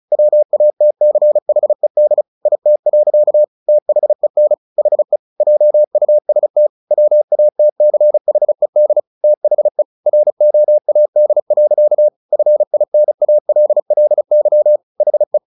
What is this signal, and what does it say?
Morse code, heard as a keyed tone, about 34 wpm, 600 Hz. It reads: WATCHED IT. THEN HE JUST WATCHED THE ROAD. FINALLY HE
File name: the old man and the sea 2 35wpm